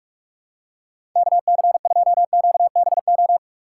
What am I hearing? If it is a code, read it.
Morse code, heard as a keyed tone, about 46 wpm, 700 Hz. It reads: KC2QBO
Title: USA Call Signs - 6 Characters - 3x Character Spacing - 45wpm